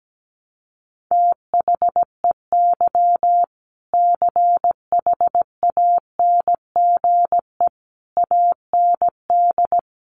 Morse code, heard as a keyed tone, about 17 wpm, 700 Hz. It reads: THEY CHANGE AND
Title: Sentences from Top 1200 Words 17wpm